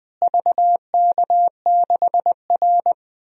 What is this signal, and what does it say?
Morse code, heard as a keyed tone, about 20 wpm, 700 Hz. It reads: VK6R
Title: DXCC-100 Call Signs - 4 Characters - Repeated 2x - 20wpm